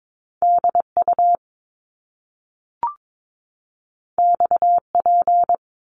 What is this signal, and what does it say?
Morse code, heard as a keyed tone, about 22 wpm, 700 Hz. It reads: DU XP
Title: Top Two Letter Combinations in Words 22wpm